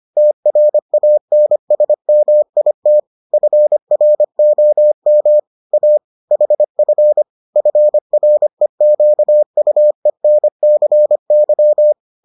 Morse code, heard as a keyed tone, about 25 wpm, 600 Hz. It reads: TRANSMIT FROM A HF FREQUENCY